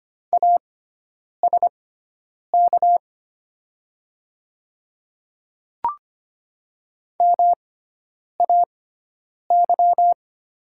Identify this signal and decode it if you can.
Morse code, heard as a keyed tone, about 25 wpm, spaced as for 9 wpm, 700 Hz. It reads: ASK MAY
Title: 6x Character Spacing - Top 200 Words - 25wpm